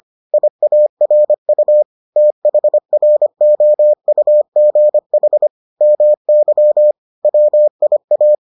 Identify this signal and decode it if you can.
Morse code, heard as a keyed tone, about 25 wpm, 600 Hz. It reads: IARU THROUGH MY WIA